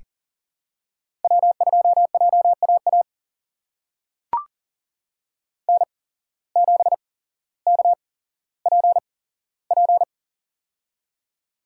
Morse code, heard as a keyed tone, about 41 wpm, spaced as for 17 wpm, 700 Hz. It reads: W2JAA N7KPP
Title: USA Call Signs - 8x Character Spacing - 40wpm